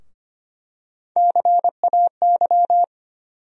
Morse code, heard as a keyed tone, about 25 wpm, 700 Hz. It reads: CAY
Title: USA Call Sign Postfixes - 3 Letters 25wpm